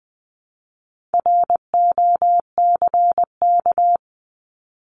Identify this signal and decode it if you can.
Morse code, heard as a keyed tone, about 20 wpm, 700 Hz. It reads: ROCK
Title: Top 500 Words Repeated 20wpm